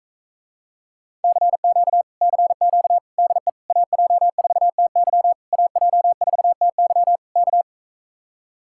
Morse code, heard as a keyed tone, about 45 wpm, 700 Hz. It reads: CQ CQ DE AJ4TY AJ4TY K